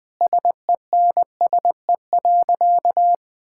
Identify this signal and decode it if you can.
Morse code, heard as a keyed tone, about 20 wpm, 700 Hz. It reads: SENSE.
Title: Sentences with 5-Word Phrases - 20wpm